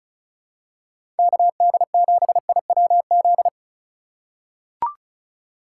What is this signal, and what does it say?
Morse code, heard as a keyed tone, about 34 wpm, 700 Hz. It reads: KD7IWZ
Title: USA Call Signs - 6 Characters - 4x Character Spacing - 35wpm